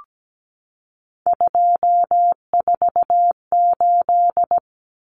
Morse code, heard as a keyed tone, about 17 wpm, 700 Hz. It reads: 248
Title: Common Cut Numbers - 3-Digit 17wpm